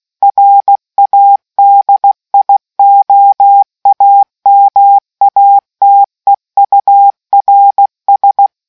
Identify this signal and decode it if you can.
Morse code, heard as a keyed tone, about 16 wpm, 800 Hz. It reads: RADIOAMATEURS